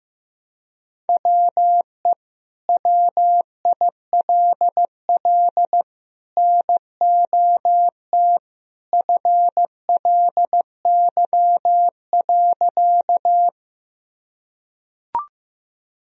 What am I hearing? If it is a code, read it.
Morse code, heard as a keyed tone, about 15 wpm, 700 Hz. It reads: WE WILL NOT FLY.
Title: Sentences from Top 1000 Words - Review Part 2 - 15wpm